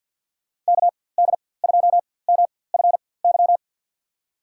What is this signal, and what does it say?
Morse code, heard as a keyed tone, about 48 wpm, spaced as for 23 wpm, 700 Hz. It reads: KD2MFY
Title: USA Call Signs - 6 Characters - 4x Character Spacing - 50wpm